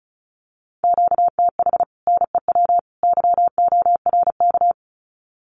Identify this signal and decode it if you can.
Morse code, heard as a keyed tone, about 34 wpm, 700 Hz. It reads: QTH NEW YORK
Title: Common QSO Phrases 35wpm